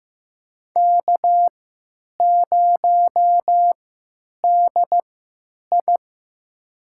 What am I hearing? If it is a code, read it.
Morse code, heard as a keyed tone, about 15 wpm, spaced as for 9 wpm, 700 Hz. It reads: K0DI
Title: USA Call Signs - 4 Characters - 3x Character Spacing - 15wpm